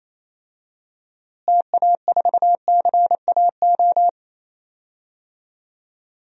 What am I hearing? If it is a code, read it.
Morse code, heard as a keyed tone, about 28 wpm, 700 Hz. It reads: TA4CAO
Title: DXCC-100 Call Signs - 6 Characters - Repeated 2x - 28wpm